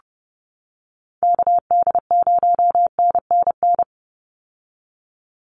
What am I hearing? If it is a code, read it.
Morse code, heard as a keyed tone, about 30 wpm, 700 Hz. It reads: KD0NNN
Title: USA Call Signs 30wpm